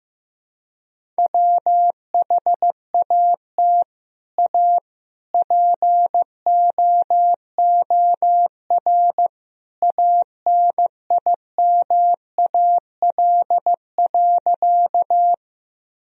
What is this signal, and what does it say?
Morse code, heard as a keyed tone, about 15 wpm, 700 Hz. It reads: WHAT A POOR ANIMAL.
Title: Sentences from Top 1000 Words - Review Part 1 - 15wpm